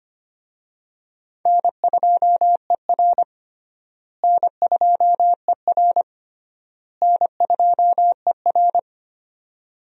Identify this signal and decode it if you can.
Morse code, heard as a keyed tone, about 25 wpm, 700 Hz. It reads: N2ER N2ER N2ER
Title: USA Call Signs - 4 Characters - Repeated 3x - 25wpm